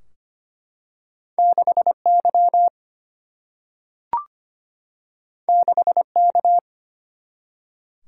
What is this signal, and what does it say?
Morse code, heard as a keyed tone, about 25 wpm, 700 Hz. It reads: 6 Y 6 K
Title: Single-Letter Single-Number 25wpm